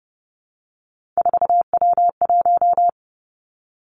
Morse code, heard as a keyed tone, about 30 wpm, 700 Hz. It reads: 4W1
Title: Three Challenging Letter-Number Combinations 30wpm